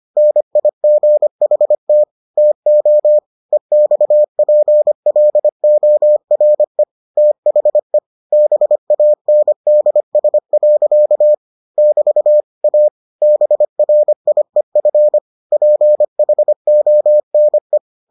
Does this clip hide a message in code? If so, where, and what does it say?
Yes: Morse code from 0.2 to 17.8 s, heard as a keyed tone, about 25 wpm, 600 Hz. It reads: NIGHT TO EXPLORE THE BANDS. <BT> A BRIEF PHONE